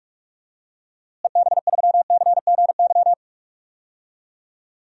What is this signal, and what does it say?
Morse code, heard as a keyed tone, about 46 wpm, 700 Hz. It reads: EB3CGY
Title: DXCC-100 Call Signs - 6 Characters - Repeated 3x - 45wpm